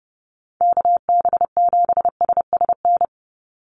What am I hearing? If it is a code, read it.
Morse code, heard as a keyed tone, about 30 wpm, 700 Hz. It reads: KB7SSN
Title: USA Call Signs 30wpm